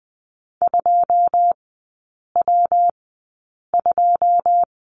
Morse code, heard as a keyed tone, about 20 wpm, spaced as for 8 wpm, 700 Hz. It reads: 2W2
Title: 114 Lesson 19e - Letter w - Three Characters 20wpm